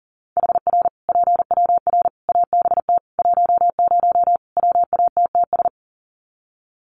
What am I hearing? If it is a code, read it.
Morse code, heard as a keyed tone, about 40 wpm, 700 Hz. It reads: HR PWR ABT 10 WATTS
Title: Common QSO Phrases 40wpm